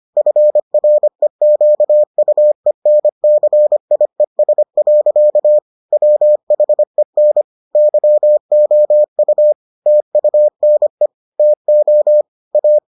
Morse code, heard as a keyed tone, about 25 wpm, 600 Hz. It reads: FREQUENCIES. WHEN YOU TUNE TO A